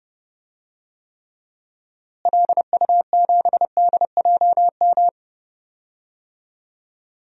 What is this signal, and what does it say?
Morse code, heard as a keyed tone, about 30 wpm, 700 Hz. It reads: LU7DJM
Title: DXCC-100 Call Signs - 6 Characters - Repeated 2x - 30wpm